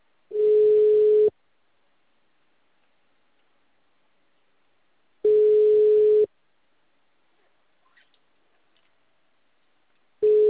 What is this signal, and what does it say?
Ringback tone (Europe, 425 Hz).
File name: out- -2007- - - .1238